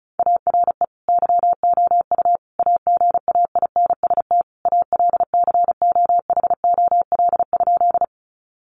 Morse code, heard as a keyed tone, about 34 wpm, 700 Hz. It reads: ARE YOU AGAINST ALCOHOL?
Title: Sentences from Top 1400 Words 35wpm